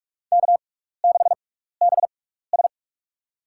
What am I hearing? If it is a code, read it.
Morse code, heard as a keyed tone, about 41 wpm, spaced as for 15 wpm, 700 Hz. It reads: K6BS
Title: USA Call Signs - 6x Character Spacing - 45wpm